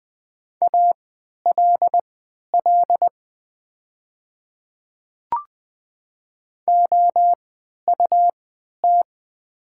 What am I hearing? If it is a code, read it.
Morse code, heard as a keyed tone, about 20 wpm, spaced as for 11 wpm, 700 Hz. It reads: ALL OUT
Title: 3x Character Spacing - Top 100 Words - 20wpm